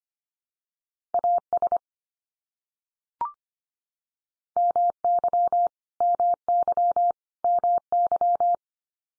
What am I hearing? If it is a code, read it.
Morse code, heard as a keyed tone, about 25 wpm, 700 Hz. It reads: AS MY MY MY
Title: Top 100 Words 25wpm x3